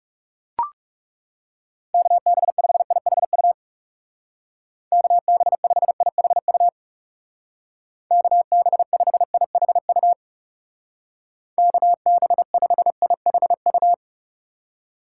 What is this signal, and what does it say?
Morse code, heard as a keyed tone, about 38 wpm, 700 Hz. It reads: KB5IHU KB5IHU KB5IHU KB5IHU